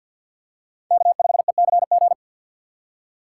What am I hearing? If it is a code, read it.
Morse code, heard as a keyed tone, about 52 wpm, 700 Hz. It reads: K5ECG
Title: USA Call Signs - 5 Characters - 2x Character Spacing - 50wpm